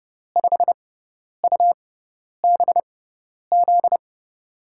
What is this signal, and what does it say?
Morse code, heard as a keyed tone, about 30 wpm, spaced as for 10 wpm, 700 Hz. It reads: 5UBZ
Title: USA Call Signs - 6x Character Spacing - 30wpm